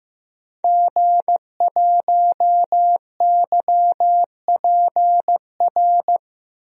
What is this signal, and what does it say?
Morse code, heard as a keyed tone, about 15 wpm, 700 Hz. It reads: G1YPR
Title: DXCC-100 Call Signs - 5 Characters - 15wpm